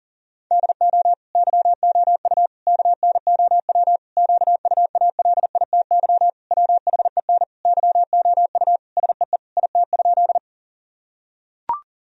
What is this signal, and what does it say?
Morse code, heard as a keyed tone, about 40 wpm, 700 Hz. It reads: DO YOU KNOW QUALITY WHEN YOU SEE IT?
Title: Sentences from Top 1000 Words - Review Part 4 - 40wpm